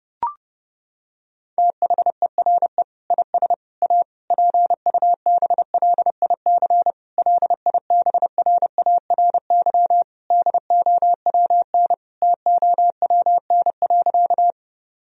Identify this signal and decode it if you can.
Morse code, heard as a keyed tone, about 32 wpm, 700 Hz. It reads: THERE IS A PUBLIC LIBRARY DOWN TOWN.
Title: Sentences from Top 1000 Words - Review Part 4 - 30wpm